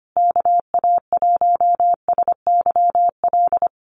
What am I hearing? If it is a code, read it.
Morse code, heard as a keyed tone, about 25 wpm, 700 Hz. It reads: KA1SYL
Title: USA Call Signs Repeated 3x 25wpm